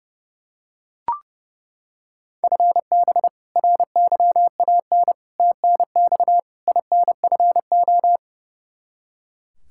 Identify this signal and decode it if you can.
Morse code, heard as a keyed tone, about 30 wpm, 700 Hz. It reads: FB RYAN TNX INFO
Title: K1USN SST QSO Practice 30wpm